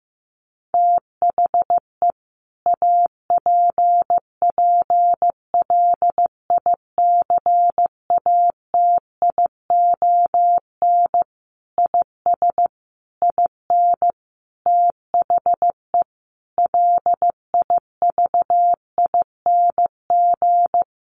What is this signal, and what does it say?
Morse code, heard as a keyed tone, about 15 wpm, 700 Hz. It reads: THE APPLICATION IS IN THE LIVING
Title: Sentences from Top 1200 Words 15wpm